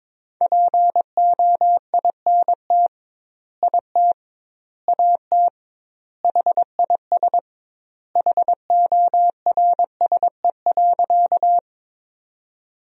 Morse code, heard as a keyed tone, about 21 wpm, 700 Hz. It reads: POINT IT AT HIS HORSE.
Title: 2x Word-Spacing - Sentences from Top 300 Words - 22wpm